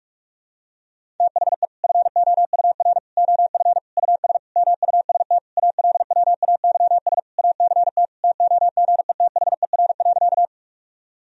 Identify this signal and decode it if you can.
Morse code, heard as a keyed tone, about 46 wpm, 700 Hz. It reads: THE FOUR OF US MUST ALWAYS ACT TOGETHER.
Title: Sentences from Top 1000 Words - Review Part 1 - 45wpm